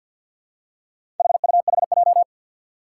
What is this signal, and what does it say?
Morse code, heard as a keyed tone, about 52 wpm, 700 Hz. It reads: HUHJ
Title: Four Challenging Letter-Number Combinations 50wpm